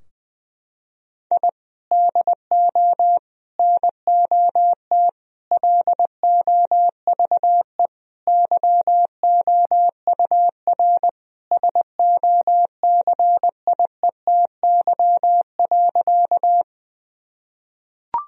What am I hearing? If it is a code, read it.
Morse code, heard as a keyed tone, about 20 wpm, 700 Hz. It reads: I DO NOT LOVE YOUR SOCIETY.
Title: Sentences from Top 1000 Words - Review Part 4 - 20wpm